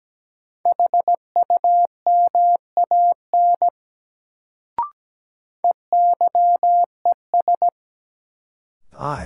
Morse code, heard as a keyed tone, about 17 wpm, 700 Hz. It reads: HUMAN EYES
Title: Sets of 5 Words - New Word Drill - 17wpm